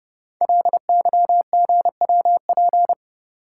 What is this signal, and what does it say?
Morse code, heard as a keyed tone, about 30 wpm, 700 Hz. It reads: LYGWP
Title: Code Groups - 5 Letter:Numbers - 30wpm